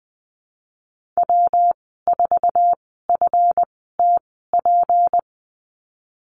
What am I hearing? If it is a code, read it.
Morse code, heard as a keyed tone, about 20 wpm, spaced as for 15 wpm, 700 Hz. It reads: W4FTP
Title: 179 Lesson 29g - Slash - Call Signs 20wpm